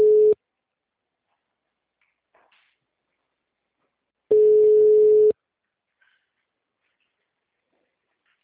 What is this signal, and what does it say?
Ringback tone (Europe, 425 Hz).